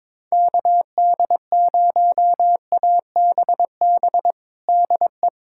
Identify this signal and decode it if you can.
Morse code, heard as a keyed tone, about 21 wpm, 700 Hz. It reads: KD0ABB DE
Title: USA Callsign DE Callsign - 6 Characters - 22wpm